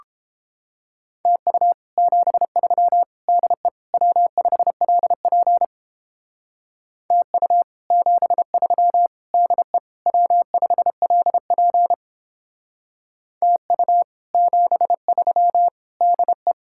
Morse code, heard as a keyed tone, about 29 wpm, 700 Hz. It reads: TU 73 DE W5LP TU 73 DE W5LP TU 73 DE